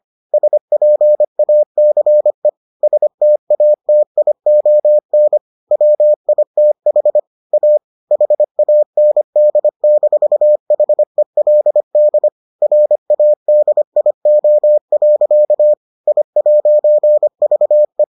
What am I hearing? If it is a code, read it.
Morse code, heard as a keyed tone, about 25 wpm, 600 Hz. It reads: SPACE STATION WITH A HAND-HELD RADIO. I'VE